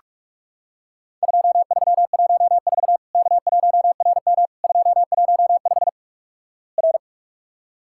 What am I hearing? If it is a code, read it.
Morse code, heard as a keyed tone, about 48 wpm, 700 Hz. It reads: 2314 K1RM 215 R
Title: Worked All Europe DX Contest - QTC Exchange - 45wpm